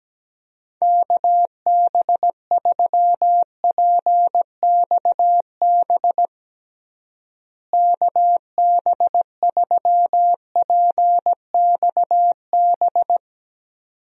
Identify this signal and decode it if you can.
Morse code, heard as a keyed tone, about 17 wpm, 700 Hz. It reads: KB3PXB KB3PXB
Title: USA Call Signs - 6 Characters - Repeated 2x - 17wpm